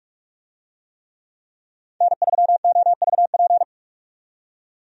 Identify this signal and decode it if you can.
Morse code, heard as a keyed tone, about 41 wpm, 700 Hz. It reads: N3OVP
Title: USA Call Signs - 5 Characters - 45wpm